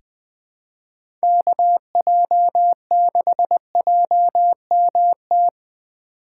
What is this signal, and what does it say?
Morse code, heard as a keyed tone, about 20 wpm, 700 Hz. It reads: KJ6JMT
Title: USA Call Signs - 6 Characters - 4x Character Spacing - 20wpm